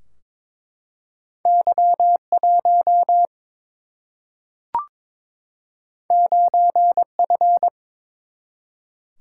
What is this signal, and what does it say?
Morse code, heard as a keyed tone, about 23 wpm, 700 Hz. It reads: Y 1 9 F
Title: Single-Letter Single-Number 22wpm